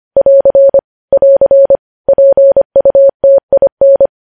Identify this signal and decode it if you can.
Morse code, heard as a keyed tone, about 25 wpm, 550 Hz. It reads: <AR> <AR> PUTIN
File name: podcast / News headlines-1x-25x25